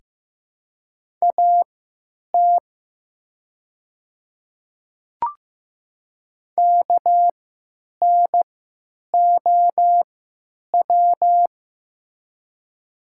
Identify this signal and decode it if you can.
Morse code, heard as a keyed tone, about 15 wpm, spaced as for 8 wpm, 700 Hz. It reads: AT KNOW
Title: 3x Character Spacing - Top 300 Words - 15wpm